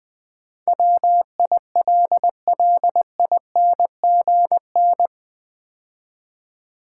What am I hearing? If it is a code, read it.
Morse code, heard as a keyed tone, about 20 wpm, 700 Hz. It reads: WILLINGN
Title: Build A Word - 4 Letters At A Time - 11-Letter Words - 20wpm